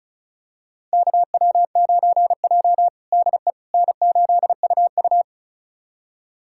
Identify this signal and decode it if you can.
Morse code, heard as a keyed tone, about 34 wpm, 700 Hz. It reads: KW9J DE N8UU
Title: USA Callsign DE Callsign - 4 Characters - 35wpm